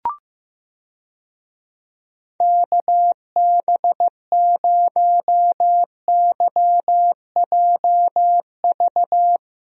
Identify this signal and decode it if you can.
Morse code, heard as a keyed tone, about 15 wpm, 700 Hz. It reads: KB0YJV
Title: USA Call Signs - 6 Characters - Repeated 2x - 15wpm